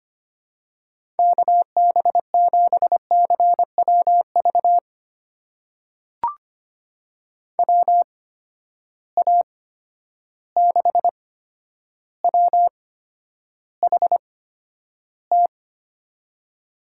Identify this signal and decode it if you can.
Morse code, heard as a keyed tone, about 25 wpm, spaced as for 11 wpm, 700 Hz. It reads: KB7CWV WA6WHT
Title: USA Call Signs - 6 Characters - 8x Character Spacing - 25wpm